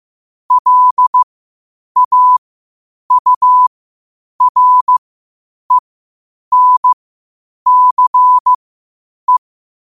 Morse code, heard as a keyed tone, about 15 wpm, spaced as for 8 wpm, 1000 Hz. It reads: LAURENCE
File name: qotdfiles / 8wpm